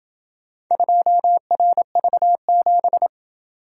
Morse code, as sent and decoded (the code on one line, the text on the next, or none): ..--- .-. ...- --...
2RV7